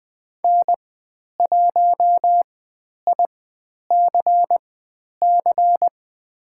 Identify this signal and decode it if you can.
Morse code, heard as a keyed tone, about 20 wpm, spaced as for 10 wpm, 700 Hz. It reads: N1ICC
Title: 109 Lesson 18g - Letter m - Call Signs 20wpm